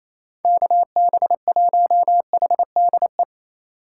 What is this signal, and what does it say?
Morse code, heard as a keyed tone, about 28 wpm, 700 Hz. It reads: KB1HDE
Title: USA Call Signs - 6 Characters - 4x Character Spacing - 28wpm